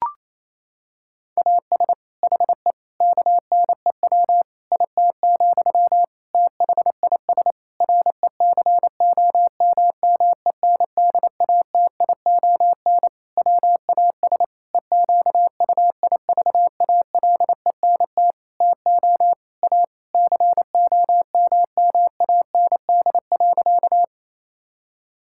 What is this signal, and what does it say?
Morse code, heard as a keyed tone, about 28 wpm, 700 Hz. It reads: AS HE KNEW IT, THIS RECOMMENDATION WAS EQUIVALENT TO A COMMAND.
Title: Sentences from Top 1600 Words 28wpm